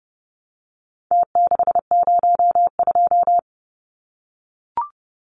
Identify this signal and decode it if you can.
Morse code, heard as a keyed tone, about 30 wpm, 700 Hz. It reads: T602
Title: Numbers - Contest Style - 4-Digit 30wpm